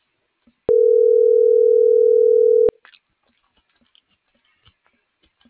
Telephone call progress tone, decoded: ringback tone

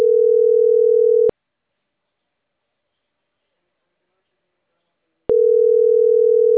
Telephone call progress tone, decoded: ringback tone

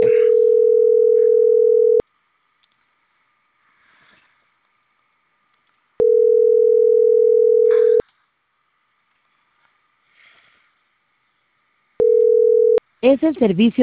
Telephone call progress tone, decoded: ringback tone

